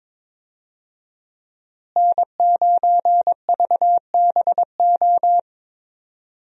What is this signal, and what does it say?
Morse code, heard as a keyed tone, about 22 wpm, 700 Hz. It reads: N9VBO
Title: USA Call Signs - 5 Characters - Repeated 3x - 22wpm